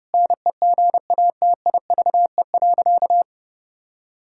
Morse code, heard as a keyed tone, about 32 wpm, 700 Hz. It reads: NEGATIVE.
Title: Sentences from Top 1000 Words - Review Part 6 - 30wpm